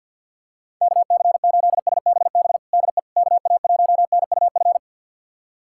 Morse code, heard as a keyed tone, about 51 wpm, 690 Hz. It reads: KC8SBB DE KA1NUF
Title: USA Callsign DE Callsign - 6 Characters - 50wpm